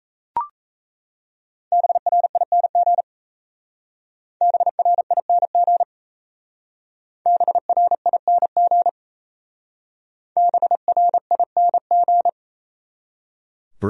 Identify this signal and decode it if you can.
Morse code, heard as a keyed tone, about 35 wpm, 700 Hz. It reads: BRING BRING BRING BRING
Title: Speed-Racing Top 500 Words 28wpm